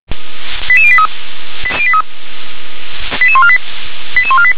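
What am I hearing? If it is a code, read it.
Five-tone selective call (ZVEI), heard as quick runs of five tones, tones of about 80 ms. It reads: 88083 88148 88148
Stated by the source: Received by SDR; ZVEI five-tone code 88511.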